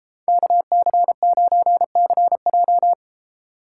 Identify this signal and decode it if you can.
Morse code, heard as a keyed tone, about 33 wpm, 700 Hz. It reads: KC9CJ